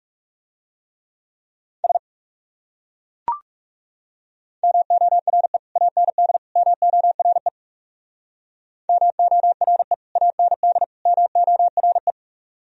Morse code, heard as a keyed tone, about 38 wpm, 700 Hz. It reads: S MORE AND MORE MORE AND MORE
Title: Speed-Racing - Sets of 3 Words 30wpm